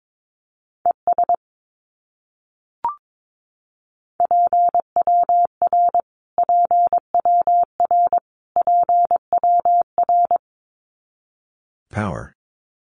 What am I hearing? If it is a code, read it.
Morse code, heard as a keyed tone, about 22 wpm, 700 Hz. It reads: ES PWR PWR PWR
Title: Abbreviations Repeated 3x - 22wpm